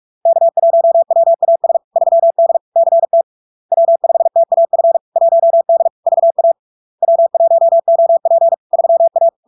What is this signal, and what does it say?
Morse code, heard as a keyed tone, about 46 wpm, 670 Hz. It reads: K1WAS 3D CT W5TAF 1D VA W1OP 3A
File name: LARK-FD-2022-Log-in-CW-45wpm-670-Hz